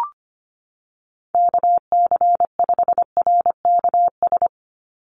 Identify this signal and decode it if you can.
Morse code, heard as a keyed tone, about 25 wpm, 700 Hz. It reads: KC5RKS
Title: USA Call Signs 25wpm